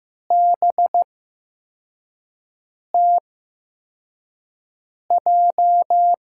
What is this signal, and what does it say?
Morse code, heard as a keyed tone, about 15 wpm, spaced as for 4 wpm, 700 Hz. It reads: BTJ
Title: USA Call Signs - 6 Characters - 8x Character Spacing - 15wpm